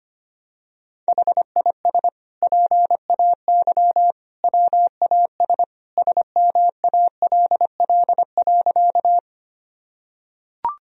Morse code, heard as a keyed tone, about 25 wpm, 700 Hz. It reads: HIS PAY WAS SMALL.
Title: Sentences from Top 1000 Words - Review Part 1 - 25wpm